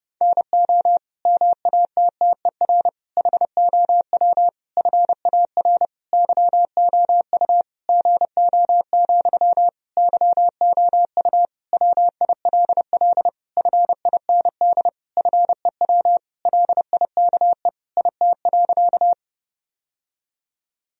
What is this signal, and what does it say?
Morse code, heard as a keyed tone, about 30 wpm, 700 Hz. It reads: NO MATTER HOW FAR YOU GO, YOU WILL FIND FEW LIKE IT.
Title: Sentences from Top 1000 Words - Review Part 1 - 30wpm